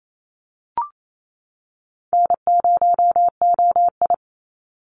Morse code, heard as a keyed tone, about 28 wpm, 700 Hz. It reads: N0OI
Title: NA SOTA Chaser Callsigns 28wpm